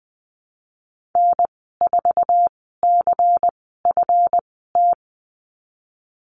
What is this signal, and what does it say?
Morse code, heard as a keyed tone, about 20 wpm, spaced as for 15 wpm, 700 Hz. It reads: N4CFT
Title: 161 Lesson 26g - Letter g - Call Signs 20wpm